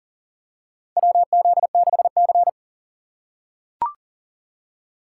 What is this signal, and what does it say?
Morse code, heard as a keyed tone, about 40 wpm, 700 Hz. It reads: WZ6C
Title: USA Call Signs - 4 Characters - 2x Character Spacing - 40wpm